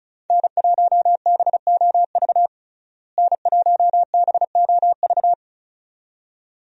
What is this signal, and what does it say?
Morse code, heard as a keyed tone, about 36 wpm, 700 Hz. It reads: N1BOV N1BOV
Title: USA Call Signs - 5 Characters - Repeated 2x - 35wpm